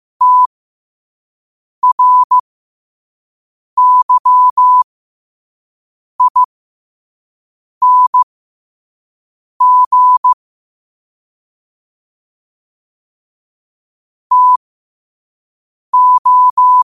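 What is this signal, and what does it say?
Morse code, heard as a keyed tone, about 15 wpm, spaced as for 5 wpm, 1000 Hz. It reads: TRYING TO
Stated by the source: Morse at 5 wpm.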